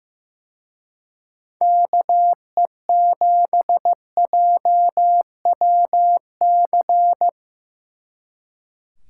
Morse code, heard as a keyed tone, about 17 wpm, 700 Hz. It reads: KE7JWC
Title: Speed Racing - USA 2x3 Call Signs 15wpm